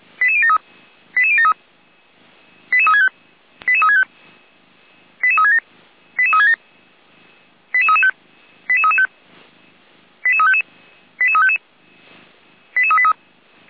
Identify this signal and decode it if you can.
Five-tone selective call (ZVEI), heard as quick runs of five tones, tones of about 70 ms. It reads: 88073 88073 88365 88365 88367 88367 88335 88335 88355 88355 88383